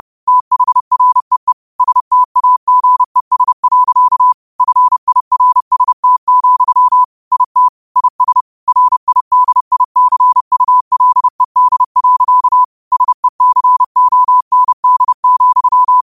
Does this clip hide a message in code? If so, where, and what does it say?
Yes: Morse code from 0.3 to 16.0 s, heard as a keyed tone, about 30 wpm, 1000 Hz. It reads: THREE STAGES. FIRST, IT IS RIDICULED. SECOND,